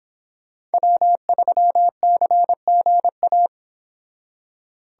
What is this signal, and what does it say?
Morse code, heard as a keyed tone, about 30 wpm, 700 Hz. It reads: W3CGA